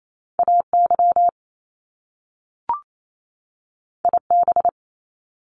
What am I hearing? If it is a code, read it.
Morse code, heard as a keyed tone, about 28 wpm, 700 Hz. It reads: AY IB
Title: Top Two Letter Combinations in Words 28wpm